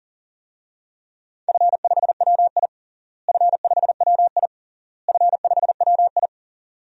Morse code, heard as a keyed tone, about 37 wpm, 700 Hz. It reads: F5WI F5WI F5WI
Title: DXCC-100 Call Signs - 4 Characters - Repeated 3x - 40wpm